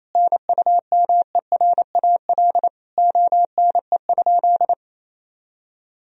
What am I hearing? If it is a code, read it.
Morse code, heard as a keyed tone, about 27 wpm, 700 Hz. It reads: NUMERAL ONE?
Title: Sentences from Top 1000 Words - Review Part 2 - 28wpm